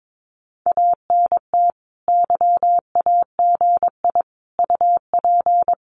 Morse code, heard as a keyed tone, about 22 wpm, 700 Hz. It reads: ANT YAGI UP
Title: Common QSO Phrases Encore 22wpm